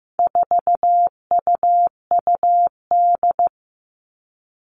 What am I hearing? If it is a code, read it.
Morse code, heard as a keyed tone, about 15 wpm, 700 Hz. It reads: 4UUD
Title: All Cut Numbers - 4-Digit 15wpm